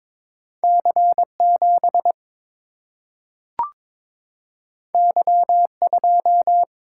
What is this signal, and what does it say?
Morse code, heard as a keyed tone, about 22 wpm, 700 Hz. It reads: C7 Y2
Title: Single-Letter Single-Number 22wpm